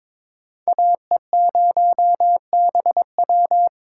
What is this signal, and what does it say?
Morse code, heard as a keyed tone, about 22 wpm, 700 Hz. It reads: AE0BW
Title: USA Call Signs - 6x Character Spacing - 22wpm